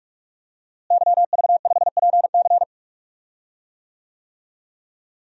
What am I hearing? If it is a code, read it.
Morse code, heard as a keyed tone, about 46 wpm, 700 Hz. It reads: YV5PC
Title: DXCC-100 Call Signs - 5 Characters - Repeated 3x - 45wpm